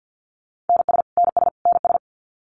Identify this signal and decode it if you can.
Morse code, heard as a keyed tone, about 49 wpm, 700 Hz. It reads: NS NS NS
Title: Canadian Provinces and Territories - Repeated 3x - 50wpm